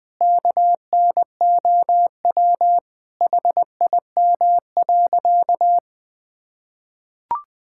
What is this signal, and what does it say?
Morse code, heard as a keyed tone, about 21 wpm, 700 Hz. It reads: KNOW HIM.
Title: Sentences with 4-Word Phrases - 20wpm